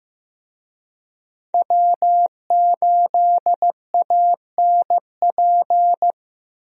Morse code, heard as a keyed tone, about 16 wpm, 700 Hz. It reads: W8ANP
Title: USA Call Signs - 5 Characters - Repeated 2x - 15wpm